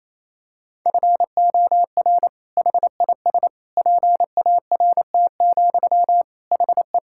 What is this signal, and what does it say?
Morse code, heard as a keyed tone, about 28 wpm, 700 Hz. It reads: FOR HIS PART, HE
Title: Sentences from Top 1000 Words - Review Part 5 - 28wpm